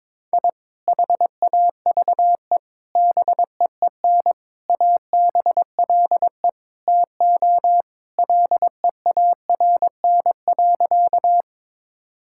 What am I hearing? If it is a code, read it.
Morse code, heard as a keyed tone, about 22 wpm, 700 Hz. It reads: I HAVE BEEN ABLE TO LEARN.
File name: practice / 22wpm / Sentences with 3-Word Phrases - 22wpm